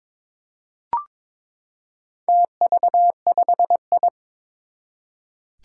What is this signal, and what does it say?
Morse code, heard as a keyed tone, about 22 wpm, 700 Hz. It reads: TV5I
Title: Four Challenging Letter-Number Combinations 22wpm